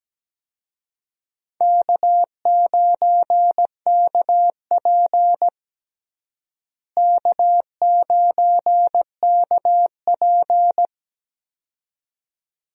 Morse code, heard as a keyed tone, about 17 wpm, 700 Hz. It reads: K9KP K9KP
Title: USA Call Signs - 4 Characters - Repeated 2x - 17wpm